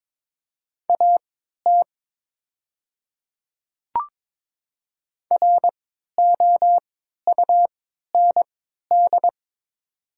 Morse code, heard as a keyed tone, about 22 wpm, spaced as for 12 wpm, 700 Hz. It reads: AT ROUND
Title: 3x Character Spacing - Top 300 Words - 22wpm